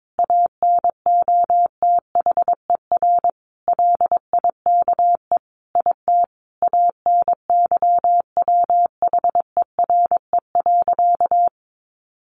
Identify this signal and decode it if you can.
Morse code, heard as a keyed tone, about 22 wpm, 700 Hz. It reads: ANOTHER LIKE IT ANYWHERE.
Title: Sentences from Top 1400 Words 22wpm